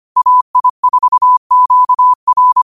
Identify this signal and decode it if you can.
Morse code, heard as a keyed tone, about 25 wpm, 1000 Hz. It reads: AI4QR
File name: qotdfiles / 25wpm